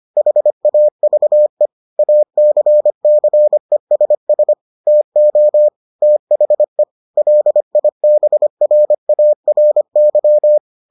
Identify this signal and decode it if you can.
Morse code, heard as a keyed tone, about 25 wpm, 600 Hz. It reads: HAVE ACCESS TO THE LIBRARY